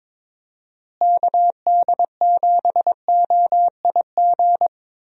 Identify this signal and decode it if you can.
Morse code, heard as a keyed tone, about 22 wpm, 700 Hz. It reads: KD7OIG
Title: USA Call Signs - 6 Characters - Repeated 3x - 22wpm